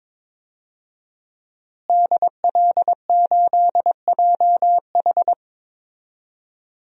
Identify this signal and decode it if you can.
Morse code, heard as a keyed tone, about 21 wpm, 700 Hz. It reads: DL8JH
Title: DXCC-100 Call Signs - 5 Characters - Repeated 3x - 22wpm